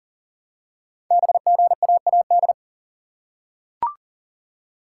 Morse code, heard as a keyed tone, about 42 wpm, 700 Hz. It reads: BGAAD
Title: Code Groups - 5 Letters - 40wpm